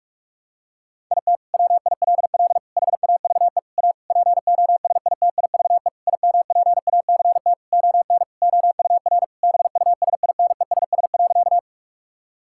Morse code, heard as a keyed tone, about 46 wpm, 700 Hz. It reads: IT WILL HAVE A POSITIVE IMPACT ON OUR BUSINESS.
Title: Sentences from Top 1000 Words - Review Part 6 - 45wpm